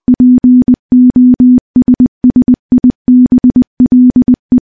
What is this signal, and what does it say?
Morse code, heard as a keyed tone, about 20 wpm, 250 Hz. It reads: POSSIBLE
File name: morse / morse-training-20WPM